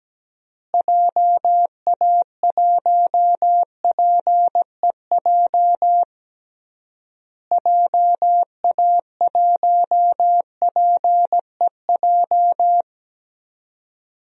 Morse code, heard as a keyed tone, about 17 wpm, 700 Hz. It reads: JA1PEJ JA1PEJ
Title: DXCC-100 Call Signs - 6 Characters - Repeated 3x - 17wpm